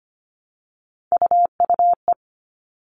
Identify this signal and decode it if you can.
Morse code, heard as a keyed tone, about 25 wpm, 700 Hz. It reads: UUE
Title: All Cut Numbers - 3-Digit 25wpm